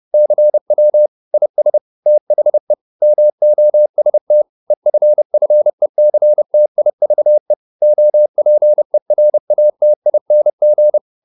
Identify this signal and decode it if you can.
Morse code, heard as a keyed tone, about 30 wpm, 600 Hz. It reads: CW IS THE MOST EFFECTIVE OPERATING